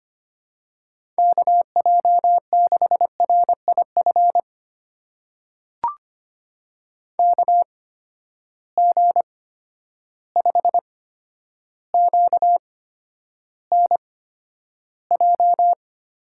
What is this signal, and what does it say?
Morse code, heard as a keyed tone, about 25 wpm, spaced as for 11 wpm, 700 Hz. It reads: KJ6RIF KG5QNJ
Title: USA Call Signs - 8x Character Spacing - 25wpm